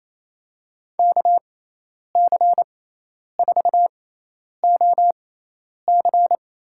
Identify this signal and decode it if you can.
Morse code, heard as a keyed tone, about 28 wpm, spaced as for 10 wpm, 700 Hz. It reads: KC4OC
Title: USA Call Signs - 6 Characters - 6x Character Spacing - 28wpm